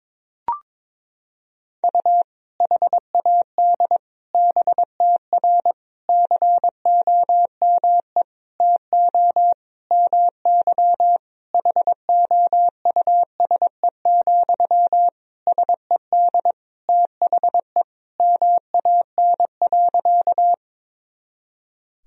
Morse code, heard as a keyed tone, about 22 wpm, 700 Hz. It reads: U HAD BTR COME TO MY HOUSE, SED THE MAN.
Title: Sentences from Top 400 Words and Abbreviations 22wpm